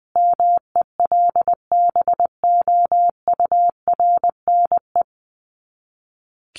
Morse code, heard as a keyed tone, about 19 wpm, 700 Hz. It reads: MELBOURNE
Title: Common QSO Phrases 20wpm